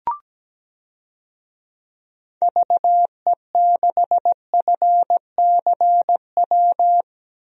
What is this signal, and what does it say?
Morse code, heard as a keyed tone, about 17 wpm, 700 Hz. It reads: VE6FCW
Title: DXCC-100 Call Signs - 6 Characters - 17wpm